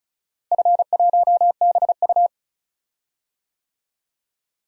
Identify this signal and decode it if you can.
Morse code, heard as a keyed tone, about 36 wpm, 700 Hz. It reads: F1BU
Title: DXCC-100 Call Signs - 4 Characters - Repeated 3x - 35wpm